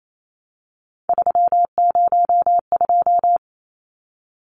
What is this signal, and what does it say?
Morse code, heard as a keyed tone, about 28 wpm, 700 Hz. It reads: 302
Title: Numbers - 3-Digit 28wpm